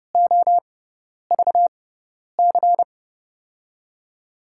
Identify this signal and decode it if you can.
Morse code, heard as a keyed tone, about 30 wpm, spaced as for 10 wpm, 700 Hz. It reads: OVC
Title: USA Call Signs - 6 Characters - 6x Character Spacing - 30wpm